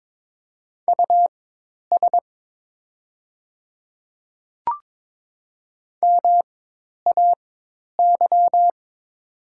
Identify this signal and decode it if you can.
Morse code, heard as a keyed tone, about 22 wpm, spaced as for 10 wpm, 690 Hz. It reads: US MAY
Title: 4x Character Spacing - Top 200 Words - 22wpm